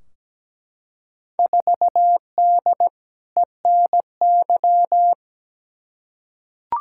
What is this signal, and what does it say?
Morse code, heard as a keyed tone, about 17 wpm, 700 Hz. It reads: 4D ENY
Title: Field Day Exchanges Encore - 17wpm